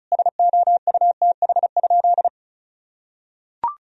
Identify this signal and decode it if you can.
Morse code, heard as a keyed tone, about 36 wpm, 700 Hz. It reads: SOUTH?
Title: Sentences from Top 1000 Words - Review Part 2 - 35wpm